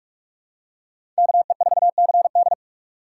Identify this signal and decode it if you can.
Morse code, heard as a keyed tone, about 46 wpm, 700 Hz. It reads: KE4CD
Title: USA Call Signs - 5 Characters - 2x Character Spacing - 45wpm